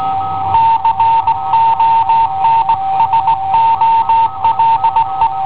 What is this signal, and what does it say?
Morse code, heard as a keyed tone, about 17 wpm, 910 Hz. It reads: CONSOLE